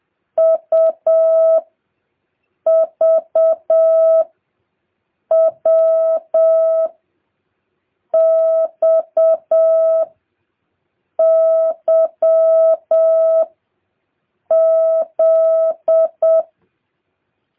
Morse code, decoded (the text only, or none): U V W X Y Z